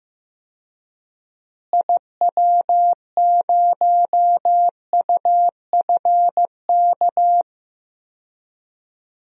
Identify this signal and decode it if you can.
Morse code, heard as a keyed tone, about 15 wpm, 700 Hz. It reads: IW0UFK
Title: DXCC-100 Call Signs - 6 Characters - Repeated 2x - 15wpm